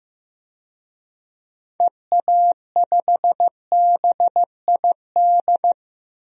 Morse code, heard as a keyed tone, about 16 wpm, 700 Hz. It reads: EA5BID
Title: DXCC-100 Call Signs - 6 Characters - Repeated 2x - 15wpm